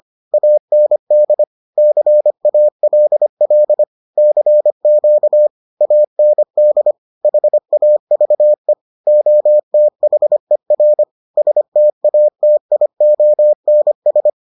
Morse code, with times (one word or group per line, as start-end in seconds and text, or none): 0.3-1.4 AND
1.8-3.8 CALL
4.2-5.5 CQ
5.8-6.9 AND
7.2-8.7 HAVE
9.1-11.0 OTHER
11.4-14.3 STATIONS